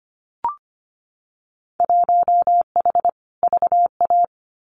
Morse code, heard as a keyed tone, about 25 wpm, 700 Hz. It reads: 1H VA
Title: Winter Field Day Exchanges 25wpm